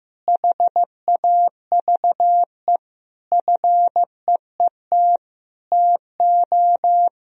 Morse code, heard as a keyed tone, about 15 wpm, 700 Hz. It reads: HAVE FEET TO
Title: Sentences from Top 1000 Words - Review Part 1 - 15wpm